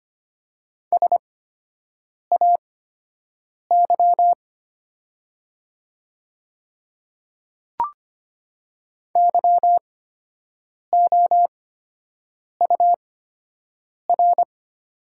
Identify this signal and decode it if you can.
Morse code, heard as a keyed tone, about 25 wpm, spaced as for 7 wpm, 700 Hz. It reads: SAY YOUR
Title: 8x Character Spacing - Top 200 Words - 25wpm